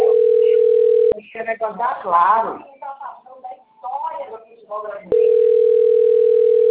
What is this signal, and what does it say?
Ringback tone (North America), 440 + 480 Hz, with speech.
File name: rg-1600-unknown- - - .80808